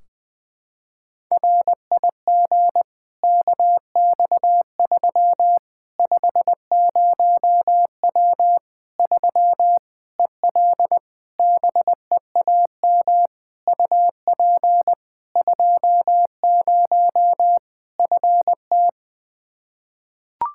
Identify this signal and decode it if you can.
Morse code, heard as a keyed tone, about 20 wpm, 700 Hz. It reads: RIG KX3 50W 3 EL BEAM UP 20 FT
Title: Practice QSO - Exchange 2 - Info - 20wpm